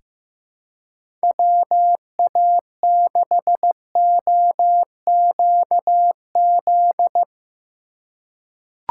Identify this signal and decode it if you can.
Morse code, heard as a keyed tone, about 16 wpm, 700 Hz. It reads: WA6OQZ
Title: USA Call Signs - 6 Characters - 6x Character Spacing - 15wpm